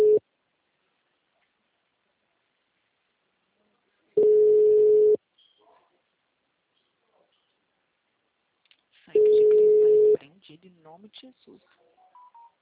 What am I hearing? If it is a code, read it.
Ringback tone (Europe, 425 Hz), with speech.
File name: out- -2010- - - .6